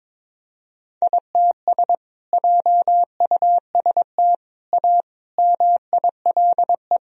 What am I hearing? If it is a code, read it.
Morse code, heard as a keyed tone, about 22 wpm, 700 Hz. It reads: ITS JUST A MILE
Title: Sentences from Top 1000 Words - Review Part 1 - 22wpm